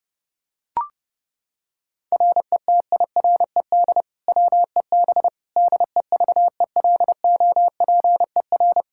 Morse code, heard as a keyed tone, about 30 wpm, 700 Hz. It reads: RETIRED WEB DEVELOPER
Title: Practice QSO - Exchange 3 - More Info - 30wpm